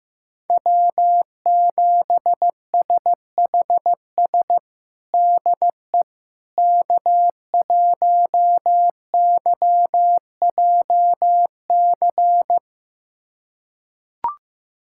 Morse code, heard as a keyed tone, about 15 wpm, 700 Hz. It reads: W7SHS DE K1YJC
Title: USA Callsign DE Callsign - 5 Characters - 15wpm